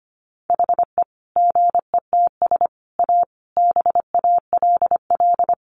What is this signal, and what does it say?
Morse code, heard as a keyed tone, about 25 wpm, 700 Hz. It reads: HE GETS A BALL
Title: Additional Easy Phrases 25wpm